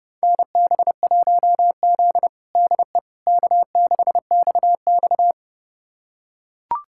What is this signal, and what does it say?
Morse code, heard as a keyed tone, about 32 wpm, 700 Hz. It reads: NB1Z DE K6XX
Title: USA Callsign DE Callsign - 4 Characters - 30wpm